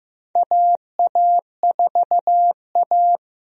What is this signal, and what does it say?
Morse code, heard as a keyed tone, about 15 wpm, 700 Hz. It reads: AA4A
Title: USA Call Signs - 4 Characters - Repeated 2x - 15wpm